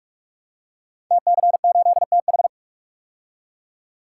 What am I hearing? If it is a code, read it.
Morse code, heard as a keyed tone, about 44 wpm, 700 Hz. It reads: TC8TH
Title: Code Groups - 5 Letter:Numbers - 45wpm